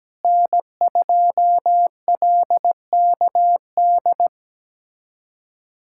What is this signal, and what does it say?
Morse code, heard as a keyed tone, about 17 wpm, 700 Hz. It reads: N2LKD
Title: USA Callsign DE Callsign - 5 Characters - 17wpm